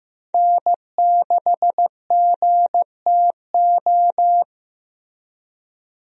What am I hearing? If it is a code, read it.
Morse code, heard as a keyed tone, about 15 wpm, 700 Hz. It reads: N6GTO
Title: USA Call Signs - 5 Characters - Repeated 3x - 15wpm